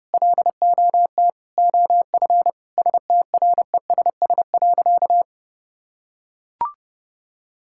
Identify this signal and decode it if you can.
Morse code, heard as a keyed tone, about 32 wpm, 700 Hz. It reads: LOT OF STRESS.
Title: Sentences from Top 1000 Words - Review Part 6 - 30wpm